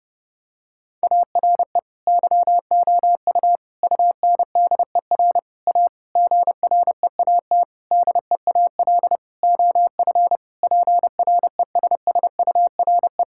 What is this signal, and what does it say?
Morse code, heard as a keyed tone, about 30 wpm, 700 Hz. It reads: ARE YOU UNDER A GREAT DEAL OF PRESSURE
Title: Sentences from Top 1000 Words - Review Part 6 - 30wpm